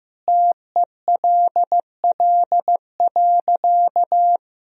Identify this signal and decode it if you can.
Morse code, heard as a keyed tone, about 15 wpm, 700 Hz. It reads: TELL.
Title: Sentences from Top 300 Words and Abbreviations 15wpm